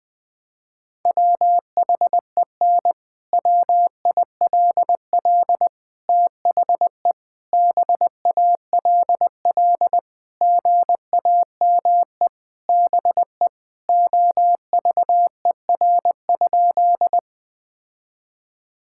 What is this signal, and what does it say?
Morse code, heard as a keyed tone, about 20 wpm, 700 Hz. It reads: WHEN WILL THE BALL GAME BE OVER?
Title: Sentences from Top 1000 Words - Review Part 6 - 20wpm